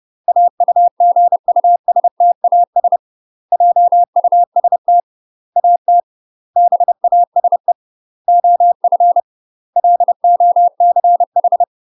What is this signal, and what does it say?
Morse code, heard as a keyed tone, about 30 wpm, 700 Hz. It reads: AUGUSTAS JUST AT BASE OF LOCH
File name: UK MASTER QSOs TRAINING / 30WPM / UK MASTER QSOs 0009